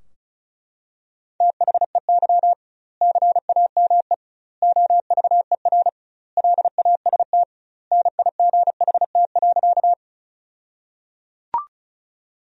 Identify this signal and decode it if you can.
Morse code, heard as a keyed tone, about 34 wpm, 700 Hz. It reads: THEY CAME OVER LAST NIGHT.
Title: 2x Word-Spacing - Sentences from Top 300 Words - 35wpm